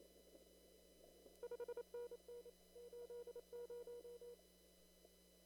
Morse code, heard as a keyed tone, about 28 wpm, 480 Hz. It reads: 5NN 80
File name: CW / kt5j wpx cw-op-k5pi